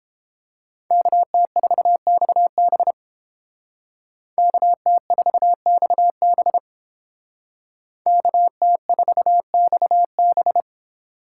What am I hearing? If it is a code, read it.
Morse code, heard as a keyed tone, about 29 wpm, 700 Hz. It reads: KT4XB KT4XB KT4XB